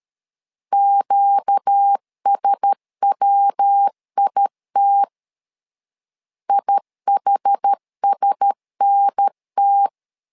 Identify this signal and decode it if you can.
Morse code, heard as a keyed tone, about 13 wpm, 800 Hz. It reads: QSWIT IHSNT
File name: LEZIONI / 4 Q1S 12wpm 1802